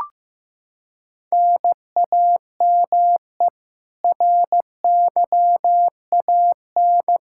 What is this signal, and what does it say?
Morse code, heard as a keyed tone, about 15 wpm, 700 Hz. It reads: NAME RYAN
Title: Practice QSO - Exchange 1 - Beginning - 15wpm